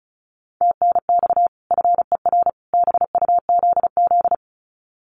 Morse code, heard as a keyed tone, about 34 wpm, 700 Hz. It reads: TNX FER BUZZ
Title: Common QSO Phrases 35wpm